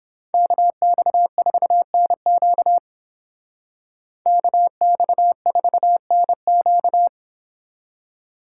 Morse code, heard as a keyed tone, about 30 wpm, 700 Hz. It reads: KX4NQ KX4NQ